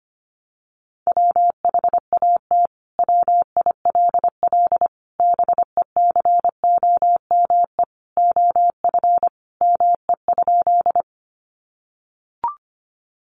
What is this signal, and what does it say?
Morse code, heard as a keyed tone, about 25 wpm, 700 Hz. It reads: WHAT WILL BECOME OF ME?
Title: Sentences from Top 600 Words 25wpm